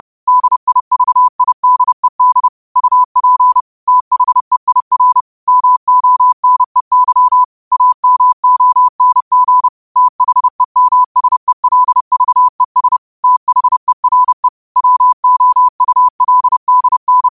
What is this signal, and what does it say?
Morse code, heard as a keyed tone, about 30 wpm, 1000 Hz. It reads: DIVIDED UP THEIR MONEY AMONG THEMSELVES THERE WOULDN
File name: qotdfiles / 30wpm